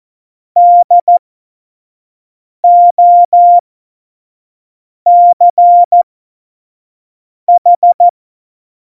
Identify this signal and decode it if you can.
Morse code, heard as a keyed tone, about 14 wpm, spaced as for 5 wpm, 700 Hz. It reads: DOCH
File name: Morsekurs / Witze / Witz 27. November 2018, 25 F70